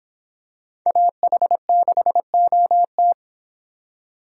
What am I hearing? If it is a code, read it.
Morse code, heard as a keyed tone, about 26 wpm, 700 Hz. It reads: AH6OT